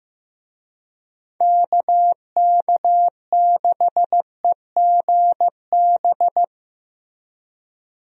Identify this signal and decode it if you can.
Morse code, heard as a keyed tone, about 15 wpm, 700 Hz. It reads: KK6EGB
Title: USA Call Signs - 6 Characters - Repeated 3x - 15wpm